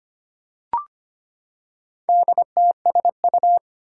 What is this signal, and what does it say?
Morse code, heard as a keyed tone, about 25 wpm, 700 Hz. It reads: DTSU
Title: Four Challenging Letter-Number Combinations 25wpm